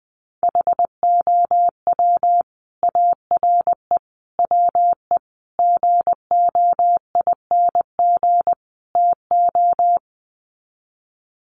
Morse code, heard as a keyed tone, about 20 wpm, 700 Hz. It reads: HOW ARE WE GOING TO
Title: Sets of 5 Words Encore 20wpm